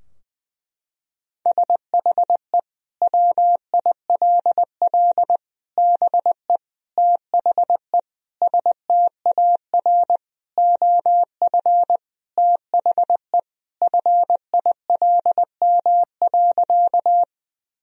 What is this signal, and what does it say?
Morse code, heard as a keyed tone, about 20 wpm, 700 Hz. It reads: SHE WILL BE THE STAR OF THE FILM.
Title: Sentences from Top 1000 Words - Review Part 3 - 20wpm